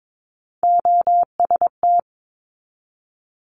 Morse code, heard as a keyed tone, about 22 wpm, 700 Hz. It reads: OST
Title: Multi-letter Phonemes 22wpm